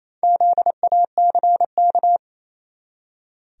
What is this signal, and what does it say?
Morse code, heard as a keyed tone, about 28 wpm, 700 Hz. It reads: ZACK
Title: USA Common Names - 28wpm